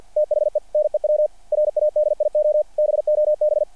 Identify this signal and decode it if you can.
Morse code, heard as a keyed tone, about 50 wpm, 600 Hz. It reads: THE NEW RADIO BOB
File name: archived / CRSnet